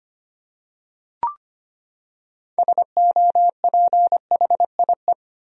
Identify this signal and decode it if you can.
Morse code, heard as a keyed tone, about 25 wpm, 700 Hz. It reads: SOPHIE
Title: USA Common Names - 25wpm